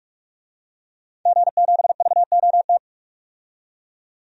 Morse code, heard as a keyed tone, about 43 wpm, 700 Hz. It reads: G7VOT
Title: DXCC-100 Call Signs - 5 Characters - 45wpm